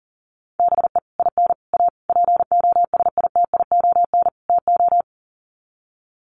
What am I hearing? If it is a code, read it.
Morse code, heard as a keyed tone, about 40 wpm, 700 Hz. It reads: BE IN A POSITION TO
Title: Sets of 5 Words Encore2 40wpm